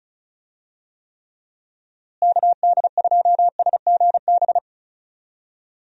Morse code, heard as a keyed tone, about 34 wpm, 700 Hz. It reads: KD2SGB
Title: USA Call Signs - 6 Characters - Repeated 2x - 35wpm